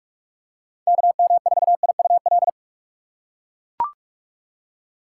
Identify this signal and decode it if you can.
Morse code, heard as a keyed tone, about 42 wpm, 700 Hz. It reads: KM4IUL
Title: USA Call Signs - 6 Characters - 4x Character Spacing - 45wpm